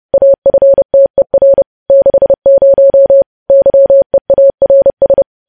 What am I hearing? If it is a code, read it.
Morse code, heard as a keyed tone, about 30 wpm, 550 Hz. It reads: AFTER 60 YEARS
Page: news headlines in morse code at 30 wpm